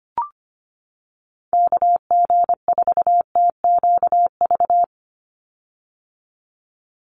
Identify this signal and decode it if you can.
Morse code, heard as a keyed tone, about 25 wpm, 700 Hz. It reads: KG4TQV
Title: USA Call Signs 25wpm